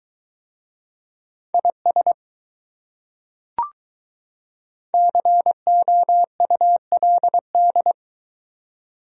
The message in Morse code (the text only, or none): IS COULD